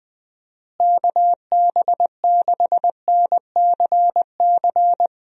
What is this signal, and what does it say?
Morse code, heard as a keyed tone, about 20 wpm, 700 Hz. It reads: KB6NCC
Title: USA Call Signs - 6 Characters - Repeated 2x - 20wpm